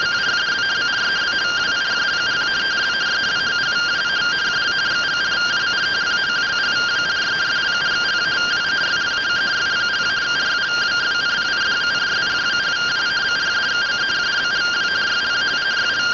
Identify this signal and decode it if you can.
Two-tone teleprinter signal, about 170 Hz shift.